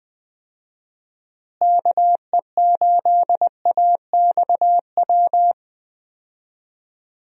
Morse code, heard as a keyed tone, about 21 wpm, 700 Hz. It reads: KE8AXW
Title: USA Call Signs - 6 Characters - Repeated 3x - 20wpm